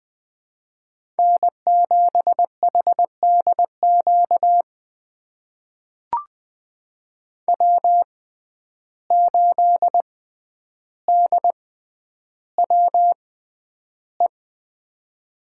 Morse code, heard as a keyed tone, about 20 wpm, spaced as for 10 wpm, 700 Hz. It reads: N7HDQ W8DWE
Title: USA Call Signs - 5 Characters - 6x Character Spacing - 20wpm